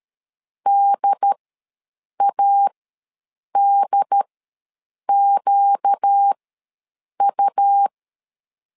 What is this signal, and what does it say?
Morse code, heard as a keyed tone, about 13 wpm, spaced as for 7 wpm, 800 Hz. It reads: DADQU
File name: LEZIONI / 4 Q ws 12wpm 1912